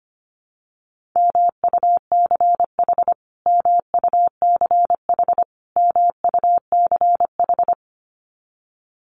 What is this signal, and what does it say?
Morse code, heard as a keyed tone, about 25 wpm, 700 Hz. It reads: MUCH MUCH MUCH
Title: Top 500 Words Repeated 25wpm